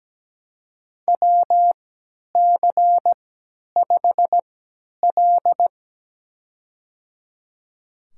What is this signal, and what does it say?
Morse code, heard as a keyed tone, about 17 wpm, spaced as for 10 wpm, 700 Hz. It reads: WC5L
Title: USA Call Signs - 4 Characters - 3x Character Spacing - 17wpm